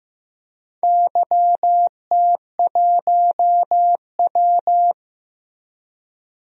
Morse code, heard as a keyed tone, about 15 wpm, 700 Hz. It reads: YT1W
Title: DXCC-100 Call Signs - 4 Characters - Repeated 2x - 15wpm